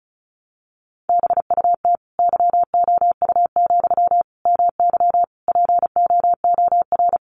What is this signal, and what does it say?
Morse code, heard as a keyed tone, about 34 wpm, 700 Hz. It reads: BUT YOU, MY POOR
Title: Sentences from Top 1000 Words 35wpm